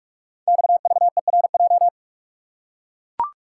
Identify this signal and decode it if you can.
Morse code, heard as a keyed tone, about 44 wpm, 690 Hz. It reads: XVERJ
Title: Code Groups - 5 Letters - 45wpm